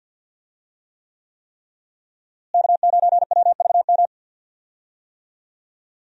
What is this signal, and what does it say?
Morse code, heard as a keyed tone, about 51 wpm, 700 Hz. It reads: K9WVM
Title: USA Call Signs - 5 Characters - Repeated 3x - 50wpm